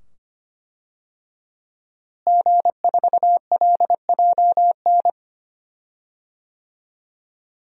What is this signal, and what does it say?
Morse code, heard as a keyed tone, about 25 wpm, 700 Hz. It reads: G4LJN
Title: DXCC-100 Call Signs - 5 Characters - Repeated 2x - 25wpm